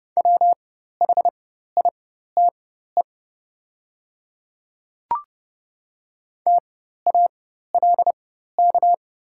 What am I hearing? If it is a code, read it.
Morse code, heard as a keyed tone, about 30 wpm, spaced as for 14 wpm, 700 Hz. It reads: WHITE TALK
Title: 4x Character Spacing - Top 500 Words - 30wpm